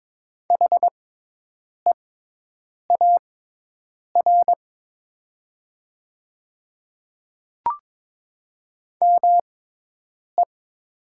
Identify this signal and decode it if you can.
Morse code, heard as a keyed tone, about 22 wpm, spaced as for 8 wpm, 700 Hz. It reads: HEAR ME
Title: 6x Character Spacing - Top 500 Words - 22wpm